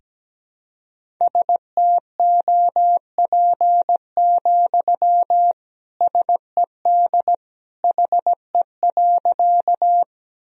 Morse code, heard as a keyed tone, about 17 wpm, 700 Hz. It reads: STOP, SED HE.
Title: Sentences from Top 300 Words and Abbreviations 17wpm